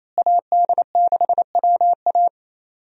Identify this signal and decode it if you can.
Morse code, heard as a keyed tone, about 28 wpm, 700 Hz. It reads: AD6WA
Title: USA Call Signs - 5 Characters - Repeated 3x - 28wpm